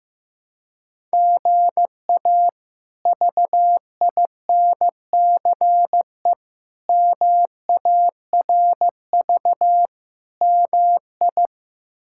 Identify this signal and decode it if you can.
Morse code, heard as a keyed tone, about 15 wpm, 700 Hz. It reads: GA VINCE MARV MI
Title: K1USN SST QSO Practice - 15wpm